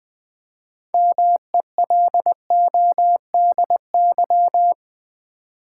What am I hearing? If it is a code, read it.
Morse code, heard as a keyed tone, about 20 wpm, 700 Hz. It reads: MELODY
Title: Build A Word - 3 Letters At A Time - 6-Letter Words - 20wpm